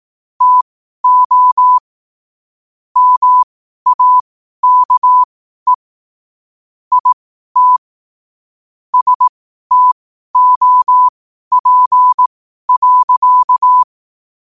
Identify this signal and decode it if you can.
Morse code, heard as a keyed tone, about 18 wpm, spaced as for 12 wpm, 1000 Hz. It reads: TO MAKE IT STOP.
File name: qotdfiles / 12wpm